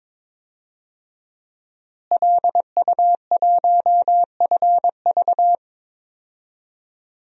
Morse code, heard as a keyed tone, about 22 wpm, 700 Hz. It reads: LU1FV
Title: DXCC-100 Call Signs - 5 Characters - Repeated 3x - 22wpm